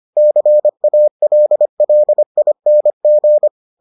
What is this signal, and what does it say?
Morse code, heard as a keyed tone, about 25 wpm, 600 Hz. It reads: CALLING